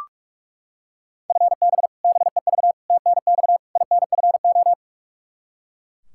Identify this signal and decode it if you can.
Morse code, heard as a keyed tone, about 41 wpm, 700 Hz. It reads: FB BEV TNX INFO
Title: Practice QSO - Exchange 3 - More Info - 45wpm